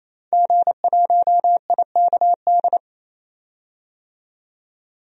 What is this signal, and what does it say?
Morse code, heard as a keyed tone, about 28 wpm, 700 Hz. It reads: G1IKD
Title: DXCC-100 Call Signs - 5 Characters - 28wpm